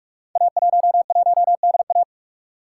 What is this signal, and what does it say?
Morse code, heard as a keyed tone, about 44 wpm, 690 Hz. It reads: A11DA
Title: Code Groups - 5 Letter:Numbers - 45wpm